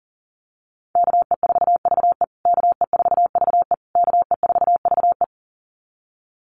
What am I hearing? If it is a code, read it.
Morse code, heard as a keyed tone, about 40 wpm, 700 Hz. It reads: KE4VE KE4VE KE4VE
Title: USA Call Signs Repeated 3x 40wpm